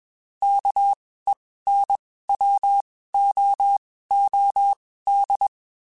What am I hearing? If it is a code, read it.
Morse code, heard as a keyed tone, about 21 wpm, spaced as for 15 wpm, 800 Hz. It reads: KENWOOD
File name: QSOs / QSOs-mp3s / WD0BGP 1@20